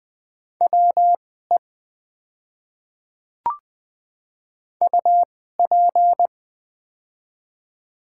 Morse code, heard as a keyed tone, about 20 wpm, spaced as for 15 wpm, 700 Hz. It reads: WE UP
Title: 2x Character Spacing - Top 100 Words - 20wpm